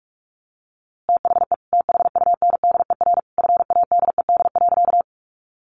Morse code, heard as a keyed tone, about 44 wpm, 700 Hz. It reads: THE THUNDER FADED.
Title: A Sound Of Thunder 45wpm